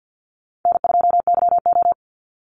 Morse code, heard as a keyed tone, about 55 wpm, 700 Hz. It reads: N2YO
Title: Easy Call Signs - 50wpm